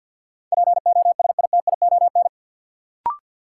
Morse code, heard as a keyed tone, about 53 wpm, 700 Hz. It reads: POSITION
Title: Sentences from Top 900 Words - New Word Drill - 50wpm